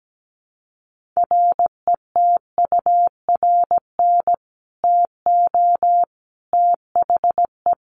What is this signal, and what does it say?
Morse code, heard as a keyed tone, about 17 wpm, 700 Hz. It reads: RETURN TO THE
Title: Sentences from Top 1400 Words 17wpm